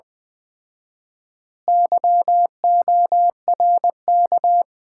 Morse code, heard as a keyed tone, about 20 wpm, 700 Hz. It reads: YORK
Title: 4x Word-Spacing - Sets of 5 Words - 20wpm